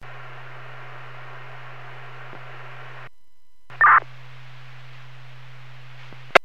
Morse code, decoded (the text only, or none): none